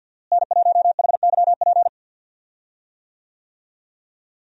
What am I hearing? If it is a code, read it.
Morse code, heard as a keyed tone, about 46 wpm, 700 Hz. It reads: N1HYP
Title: USA Call Signs - 5 Characters - Repeated 2x - 50wpm